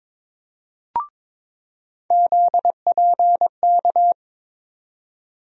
Morse code, heard as a keyed tone, about 22 wpm, 700 Hz. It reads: ZPK
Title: USA Call Sign Postfixes - 3 Letters 22wpm